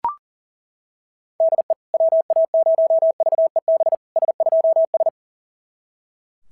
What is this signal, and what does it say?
Morse code, heard as a keyed tone, about 40 wpm, 690 Hz. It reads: DE WA0VEB S2S
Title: SOTA QSO Practice 40wpm